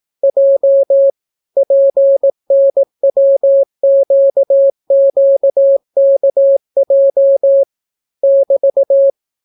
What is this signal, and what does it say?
Morse code, heard as a keyed tone, about 18 wpm, 550 Hz. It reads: J PNWQQKJ <BT>